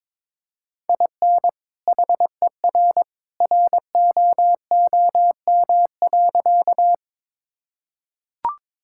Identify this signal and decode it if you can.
Morse code, heard as a keyed tone, about 22 wpm, 700 Hz. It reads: IN HER ROOM.
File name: practice / 22wpm / Sentences with 2-Word Phrases - 22wpm